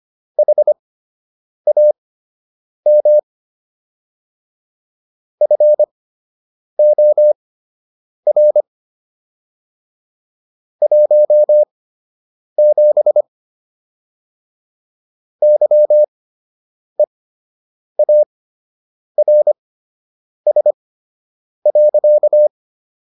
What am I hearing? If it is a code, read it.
Morse code, heard as a keyed tone, about 25 wpm, spaced as for 8 wpm, 600 Hz. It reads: HAM FOR 17 YEARS.